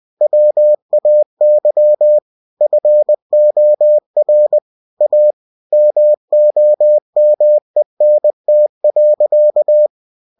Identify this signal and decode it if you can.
Morse code, heard as a keyed tone, about 20 wpm, 600 Hz. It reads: WAY FOR A MOMENT.
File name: cw-books / winnie-the-pooh / winnie-20wpm-page-0064